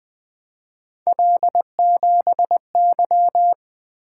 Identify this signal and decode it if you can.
Morse code, heard as a keyed tone, about 20 wpm, 700 Hz. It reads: L7Y
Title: Three Letter-Number Combinations - 20wpm